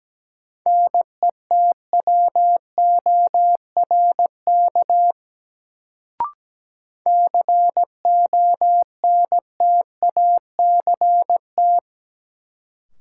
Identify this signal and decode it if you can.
Morse code, heard as a keyed tone, about 17 wpm, 700 Hz. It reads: NETWORK CONTACT
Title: Sentences from Top 1000 Words - Review Part 7 - New Word Drill - 17wpm